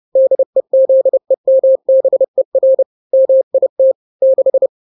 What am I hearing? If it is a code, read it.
Morse code, heard as a keyed tone, about 30 wpm, 530 Hz. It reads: DEZEMBER MIT 6
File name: daily / 29wpm